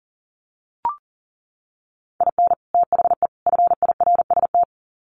Morse code, heard as a keyed tone, about 40 wpm, 700 Hz. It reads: IN THE FIRST
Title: Sets of 3 Words 40wpm